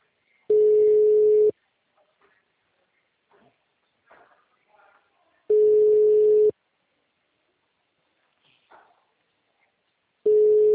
Ringback tone (Europe, 425 Hz).